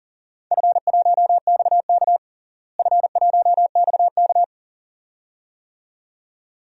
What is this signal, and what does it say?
Morse code, heard as a keyed tone, about 40 wpm, 700 Hz. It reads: F1XK F1XK
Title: DXCC-100 Call Signs - 4 Characters - Repeated 2x - 40wpm